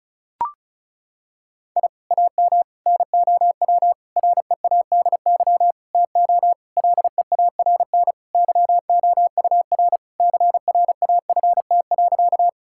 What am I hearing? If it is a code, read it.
Morse code, heard as a keyed tone, about 36 wpm, 700 Hz. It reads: I AM NOW READY TO LEARN YOUR CRAFT.
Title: Sentences from Top 1000 Words - Review Part 6 - 35wpm